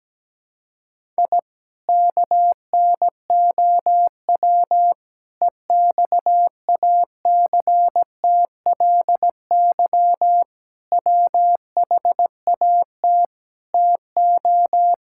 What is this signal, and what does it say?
Morse code, heard as a keyed tone, about 17 wpm, 700 Hz. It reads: I KNOW EXACTLY WHAT TO
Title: Sentences from Top 1000 Words - Review Part 6 - 17wpm